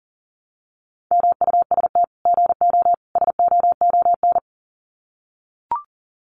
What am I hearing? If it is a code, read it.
Morse code, heard as a keyed tone, about 40 wpm, 700 Hz. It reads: MUST GO SOON
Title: Common QSO Phrases 40wpm